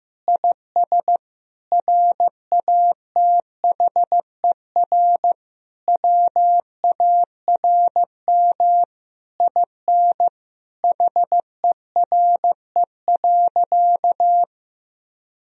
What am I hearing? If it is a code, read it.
Morse code, heard as a keyed tone, about 15 wpm, 700 Hz. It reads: IS RATHER WARM IN HERE.
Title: Sentences from Top 1000 Words - Review Part 4 - 15wpm